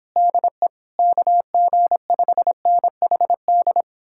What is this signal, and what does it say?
Morse code, heard as a keyed tone, about 29 wpm, 700 Hz. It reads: DE KG5NHD